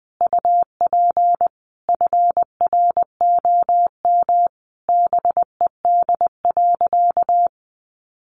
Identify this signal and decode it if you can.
Morse code, heard as a keyed tone, about 20 wpm, 700 Hz. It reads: UP FROM BED.
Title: Sentences from Top 900 Words 20wpm